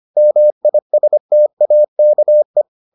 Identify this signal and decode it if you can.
Morse code, heard as a keyed tone, about 25 wpm, 600 Hz. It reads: MISTAKE